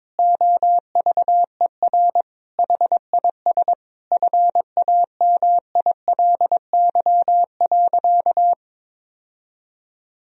Morse code, heard as a keyed tone, about 22 wpm, 700 Hz. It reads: OVER HIS FAMILY.
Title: Sentences from Top 1000 Words - Review Part 3 - 22wpm